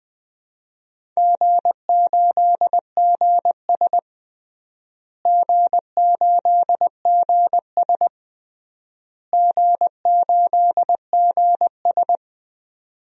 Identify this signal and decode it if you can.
Morse code, heard as a keyed tone, about 20 wpm, 700 Hz. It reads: G8GS G8GS G8GS
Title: DXCC-100 Call Signs - 4 Characters - Repeated 3x - 20wpm